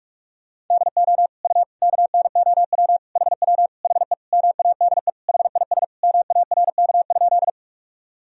Morse code, heard as a keyed tone, about 47 wpm, 700 Hz. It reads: DO U KNOW HW HE MADE HIS MARK?
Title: Sentences from Top 300 Words and Abbreviations 45wpm